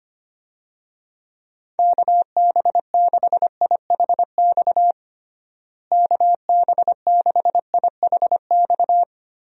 Morse code, heard as a keyed tone, about 25 wpm, 700 Hz. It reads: KB6IHX KB6IHX
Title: USA Call Signs - 6 Characters - Repeated 3x - 25wpm